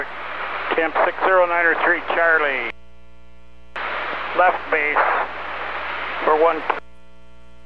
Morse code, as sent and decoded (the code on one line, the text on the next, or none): none